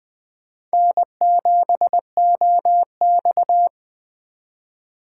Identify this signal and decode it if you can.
Morse code, heard as a keyed tone, about 21 wpm, 700 Hz. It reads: N7OX
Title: USA Call Signs - 4 Characters - 3x Character Spacing - 20wpm